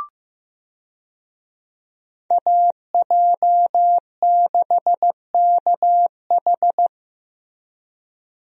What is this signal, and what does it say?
Morse code, heard as a keyed tone, about 15 wpm, 700 Hz. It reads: AJ6KH
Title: USA Call Signs - 5 Characters - Repeated 3x - 15wpm